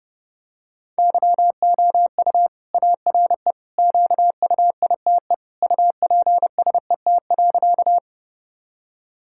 Morse code, heard as a keyed tone, about 30 wpm, 700 Hz. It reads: YOU ARE QUITE UPSET.
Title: Sentences from Top 1000 Words - Review Part 5 - 30wpm